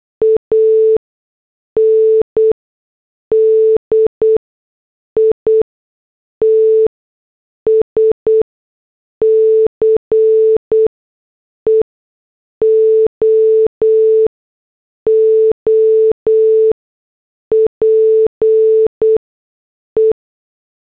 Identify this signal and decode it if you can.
Morse code, heard as a keyed tone, about 8 wpm, spaced as for 6 wpm, 440 Hz. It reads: AND ITS CEO OPE
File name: morse / headlines